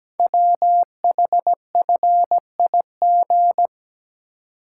Morse code, heard as a keyed tone, about 17 wpm, 700 Hz. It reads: WHFIG
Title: Code Groups - 5 Letters - 17wpm